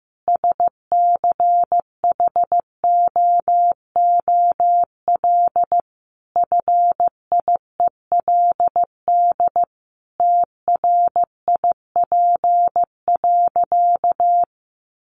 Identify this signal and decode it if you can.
Morse code, heard as a keyed tone, about 15 wpm, 700 Hz. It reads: SCHOOL FIELD TRIP.
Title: Additional Easy Phrases 15wpm